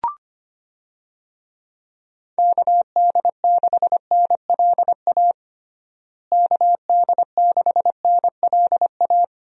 Morse code, heard as a keyed tone, about 25 wpm, 700 Hz. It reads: KD6NLA KD6NLA
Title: USA Call Signs - 6 Characters - Repeated 2x - 25wpm